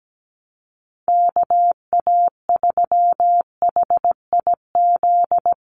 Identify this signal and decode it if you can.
Morse code, heard as a keyed tone, about 17 wpm, 700 Hz. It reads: KA3HIZ
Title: USA Call Signs 17wpm